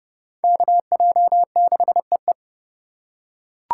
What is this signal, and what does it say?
Morse code, heard as a keyed tone, about 30 wpm, 700 Hz. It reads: KJ6EE
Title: USA Call Signs - 5 Characters - 3x Character Spacing - 30wpm